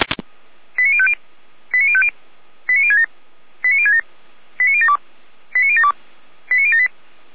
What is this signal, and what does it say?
Five-tone selective call (ZVEI), heard as quick runs of five tones, tones of about 70 ms. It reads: 89050 89050 89067 89067 89072 89072 89078
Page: /455/SDR/ZVEI/89012/